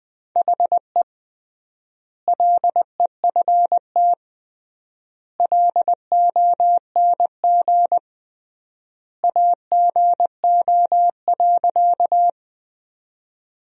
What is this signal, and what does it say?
Morse code, heard as a keyed tone, about 20 wpm, 700 Hz. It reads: HE LEFT LONG AGO.
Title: 3x Word-Spacing - Sentences from Top 500 Words - 20wpm